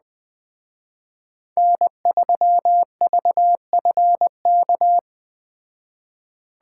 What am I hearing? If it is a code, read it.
Morse code, heard as a keyed tone, about 20 wpm, 700 Hz. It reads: N3VFK